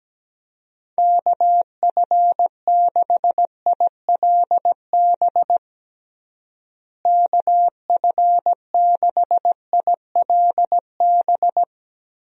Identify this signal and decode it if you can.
Morse code, heard as a keyed tone, about 17 wpm, 700 Hz. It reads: KF6ILB KF6ILB
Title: USA Call Signs - 6 Characters - Repeated 2x - 17wpm